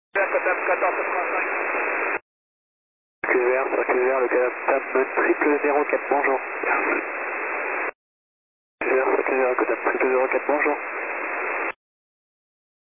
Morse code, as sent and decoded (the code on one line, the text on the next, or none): none